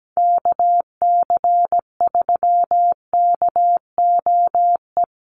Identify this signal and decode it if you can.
Morse code, heard as a keyed tone, about 17 wpm, 700 Hz. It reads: KC3KOE
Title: USA Call Signs Repeated 3x 17wpm